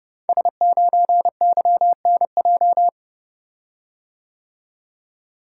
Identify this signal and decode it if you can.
Morse code, heard as a keyed tone, about 30 wpm, 700 Hz. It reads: S9YNJ
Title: Code Groups - 5 Letter:Numbers - 30wpm